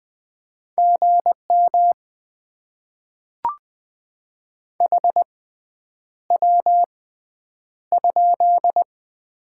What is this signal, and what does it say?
Morse code, heard as a keyed tone, about 20 wpm, spaced as for 9 wpm, 700 Hz. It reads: GM HW?
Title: Top 20 QSO Elements - 6x Character Spacing - 20wpm